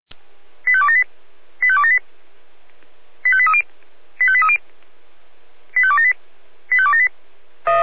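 Five-tone selective call (ZVEI), heard as quick runs of five tones, tones of about 70 ms. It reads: 85278 85278 85830 85830 85278 85278
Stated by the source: ZVEI five-tone call received by SDR.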